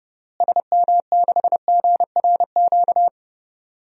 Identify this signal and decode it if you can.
Morse code, heard as a keyed tone, about 30 wpm, 700 Hz. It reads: SM6GRQ
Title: DXCC-100 Call Signs - 6 Characters - Repeated 2x - 30wpm